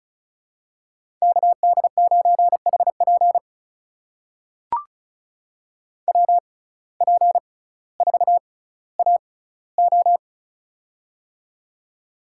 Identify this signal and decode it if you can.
Morse code, heard as a keyed tone, about 36 wpm, spaced as for 18 wpm, 700 Hz. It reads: KD9HP WP4AO
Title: USA Call Signs - 5 Characters - 6x Character Spacing - 35wpm